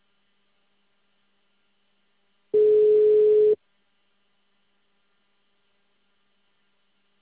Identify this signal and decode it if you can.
Ringback tone (Europe, 425 Hz).
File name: out- -2044- - - .24209